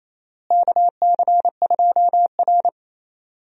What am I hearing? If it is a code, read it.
Morse code, heard as a keyed tone, about 28 wpm, 700 Hz. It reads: KC2R
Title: USA Call Signs - 4 Characters - Repeated 2x - 28wpm